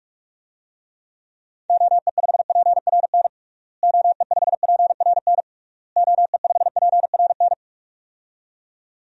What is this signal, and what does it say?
Morse code, heard as a keyed tone, about 46 wpm, 700 Hz. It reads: OE5PRN OE5PRN OE5PRN
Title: DXCC-100 Call Signs - 6 Characters - Repeated 3x - 45wpm